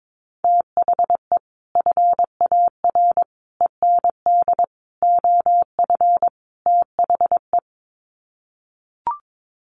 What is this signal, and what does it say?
Morse code, heard as a keyed tone, about 22 wpm, 700 Hz. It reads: THE FAR END OF THE
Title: Sets of 5 Words 22wpm